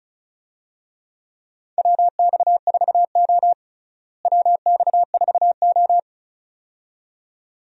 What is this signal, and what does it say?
Morse code, heard as a keyed tone, about 34 wpm, 700 Hz. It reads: WX4O WX4O
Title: USA Call Signs - 4 Characters - Repeated 2x - 35wpm